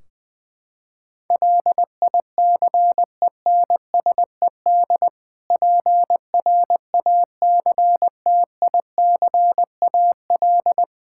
Morse code, heard as a keyed tone, about 21 wpm, 700 Hz. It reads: LICENSED PRACTICAL
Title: Practice QSO - Exchange 3 - More Info - 20wpm